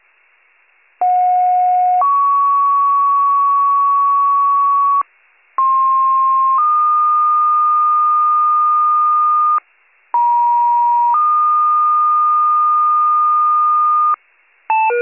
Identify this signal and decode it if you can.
Two-tone page, heard as a one-second tone, then a three-second tone, about 730 then 1120 Hz, about 1060 then 1220 Hz, about 950 then 1220 Hz. It is heard from 1.0 to 14.1 s.